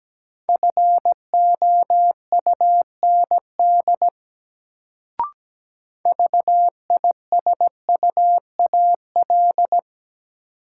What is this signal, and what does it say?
Morse code, heard as a keyed tone, about 17 wpm, 700 Hz. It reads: FOUND VISUAL
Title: Sets of 5 Words Encore - New Word Drill - 17wpm